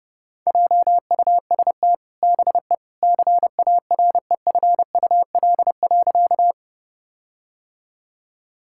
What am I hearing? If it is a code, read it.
Morse code, heard as a keyed tone, about 30 wpm, 700 Hz. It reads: JUST BE CAREFUL.
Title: Sentences from Top 1600 Words 30wpm